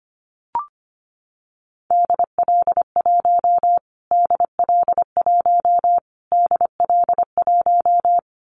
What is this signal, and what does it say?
Morse code, heard as a keyed tone, about 25 wpm, 700 Hz. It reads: DL1 DL1 DL1
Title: Callsign Prefixes for Top 25 DXCC - Repeated 3x - 25wpm